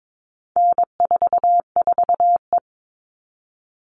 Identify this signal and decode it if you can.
Morse code, heard as a keyed tone, about 22 wpm, 700 Hz. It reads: N44E
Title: All Cut Numbers - 4-Digit 22wpm